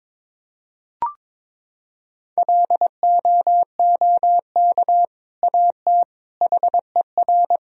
Morse code, heard as a keyed tone, about 22 wpm, 700 Hz. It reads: LOOK AT HER
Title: Sentences from Top 1000 Words - Review Part 2 - 22wpm